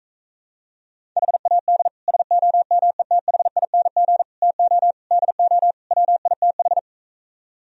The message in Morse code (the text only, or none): HAD SOMETHING TO DO WITH